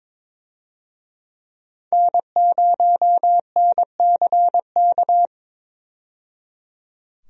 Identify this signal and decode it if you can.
Morse code, heard as a keyed tone, about 22 wpm, 700 Hz. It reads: N0NCK
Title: USA Call Signs - 5 Characters - 22wpm